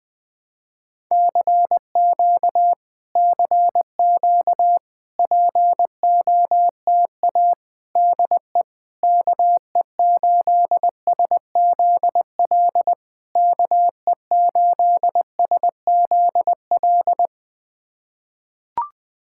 Morse code, heard as a keyed tone, about 20 wpm, 700 Hz. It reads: CQ CQ POTA DE KE8SZL KE8SZL
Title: POTA QSO Practice - 20wpm